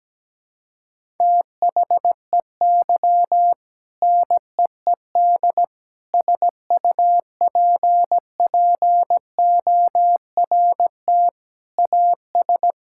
Morse code, heard as a keyed tone, about 17 wpm, 700 Hz. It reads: THEY NEED SUPPORT AS
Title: Sentences from Top 1000 Words - Review Part 5 - 17wpm